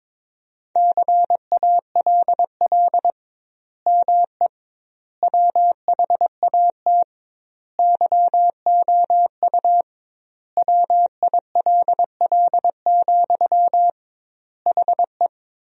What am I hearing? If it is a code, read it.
Morse code, heard as a keyed tone, about 22 wpm, 700 Hz. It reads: CALL ME WHAT YOU WILL, HE
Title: 2x Word-Spacing - Sentences from Top 200 Words - 22wpm